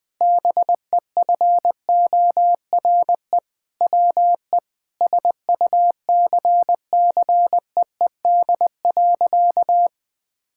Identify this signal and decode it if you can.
Morse code, heard as a keyed tone, about 20 wpm, 700 Hz. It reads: BEFORE WE SUCCEED.
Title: Sentences with 6-Word Phrases - Spoken Beginning - 20wpm